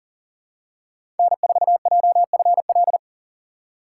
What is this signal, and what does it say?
Morse code, heard as a keyed tone, about 40 wpm, 700 Hz. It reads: N4JFL
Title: USA Call Signs - 5 Characters - 2x Character Spacing - 40wpm